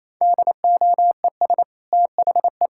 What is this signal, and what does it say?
Morse code, heard as a keyed tone, about 28 wpm, 700 Hz. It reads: DOES THE
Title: Sentences from Top 1600 Words 28wpm